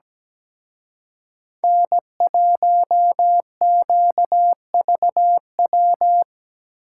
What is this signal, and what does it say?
Morse code, heard as a keyed tone, about 18 wpm, 700 Hz. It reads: N1QVW
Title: Speed Racing - USA 1x3 Call Signs 17wpm